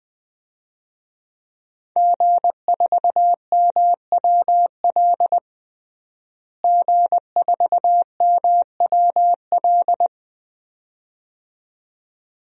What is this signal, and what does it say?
Morse code, heard as a keyed tone, about 20 wpm, 700 Hz. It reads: G4MWL G4MWL
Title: DXCC-100 Call Signs - 5 Characters - Repeated 2x - 20wpm